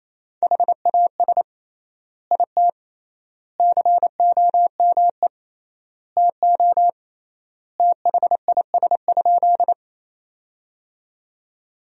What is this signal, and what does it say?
Morse code, heard as a keyed tone, about 28 wpm, 700 Hz. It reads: HAS IT COME TO THIS?
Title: 3x Word-Spacing - Sentences from Top 100 Words - 28wpm